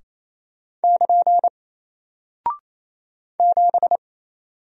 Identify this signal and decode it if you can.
Morse code, heard as a keyed tone, about 27 wpm, 700 Hz. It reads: <KN> 7
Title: Single Character (Letter:Number:Punctuation:Prosign) 28wpm